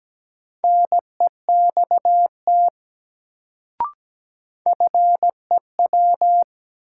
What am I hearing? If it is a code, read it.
Morse code, heard as a keyed tone, about 17 wpm, 700 Hz. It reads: NEXT FEW
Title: Sets of 2 Words - New Word Drill - 17wpm